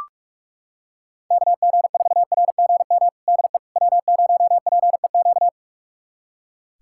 Morse code, heard as a keyed tone, about 43 wpm, 700 Hz. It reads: KG4RGM DE W0PEQ
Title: USA Callsign DE Callsign 45wpm